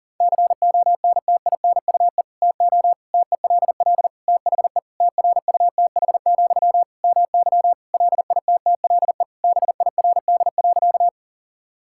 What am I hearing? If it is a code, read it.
Morse code, heard as a keyed tone, about 37 wpm, 700 Hz. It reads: CONTINUE TO TELL THE TRUTH, MY LITTLE BIRD.
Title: Sentences from Top 1000 Words - Review Part 6 - 40wpm